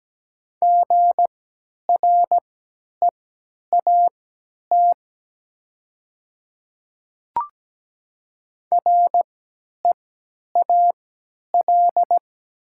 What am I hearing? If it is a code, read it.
Morse code, heard as a keyed tone, about 17 wpm, spaced as for 10 wpm, 700 Hz. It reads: GREAT REAL
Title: 3x Character Spacing - Top 500 Words - 17wpm